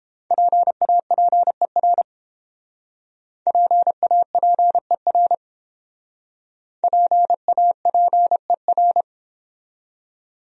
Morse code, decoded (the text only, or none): PAPER PAPER PAPER